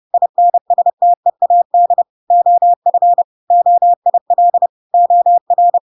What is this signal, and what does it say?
Morse code, heard as a keyed tone, about 30 wpm, 700 Hz. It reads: INSTEAD OF OIL OR
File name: Pinocchio / Pinocchio ch5 30 30 ik1hge